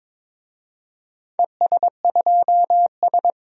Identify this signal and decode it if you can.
Morse code, heard as a keyed tone, about 22 wpm, 700 Hz. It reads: ES2S
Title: DXCC-100 Call Signs - 4 Characters - 22wpm